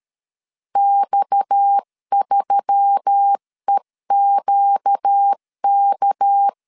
Morse code, heard as a keyed tone, about 13 wpm, 800 Hz. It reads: X3EQK
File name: CW / DET / 1 6 1500C S C 12wpm 3533